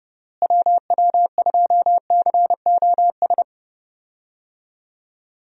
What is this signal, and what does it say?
Morse code, heard as a keyed tone, about 30 wpm, 700 Hz. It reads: WW2COS
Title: USA Call Signs - 6 Characters - Repeated 3x - 30wpm